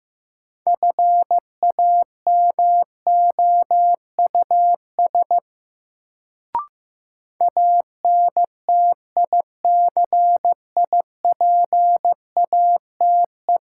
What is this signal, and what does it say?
Morse code, heard as a keyed tone, about 15 wpm, 700 Hz. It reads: FAMOUS ANTICIPATE
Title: Sentences from Top 1500 Words - New Word Drill Alt - 15wpm